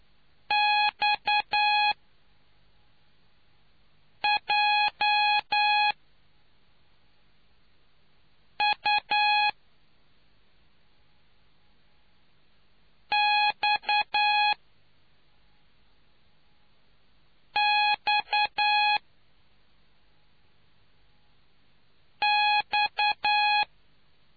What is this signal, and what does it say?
Morse code, heard as a keyed tone, about 10 wpm, 840 Hz. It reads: X J U X X X